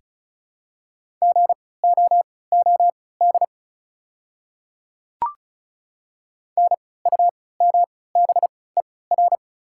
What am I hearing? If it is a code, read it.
Morse code, heard as a keyed tone, about 36 wpm, spaced as for 20 wpm, 690 Hz. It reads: GOOD NUMBER
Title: 3x Character Spacing - Top 200 Words - 35wpm